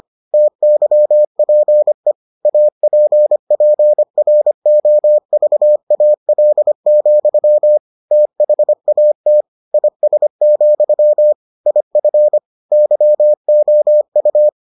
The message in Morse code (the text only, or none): TYPE APPROVAL, THAT IS, IF YOU